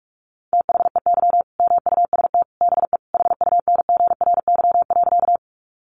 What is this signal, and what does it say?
Morse code, heard as a keyed tone, about 44 wpm, 700 Hz. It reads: THEY MUST BE HUNGRY.
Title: Sentences from Top 1400 Words 45wpm